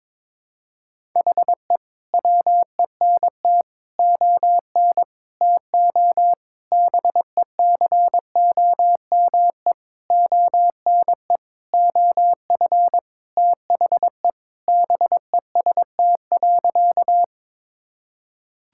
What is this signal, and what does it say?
Morse code, heard as a keyed tone, about 21 wpm, 700 Hz. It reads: HE WENT ON TO BECOME ONE OF THE BEST.
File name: Sentences with 7-Word Phrases - 22wpm